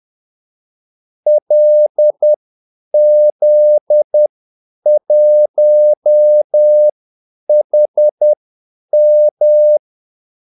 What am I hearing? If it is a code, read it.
Morse code, heard as a keyed tone, about 10 wpm, spaced as for 8 wpm, 600 Hz. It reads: LZ1HM
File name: Callsigns-10-8wpm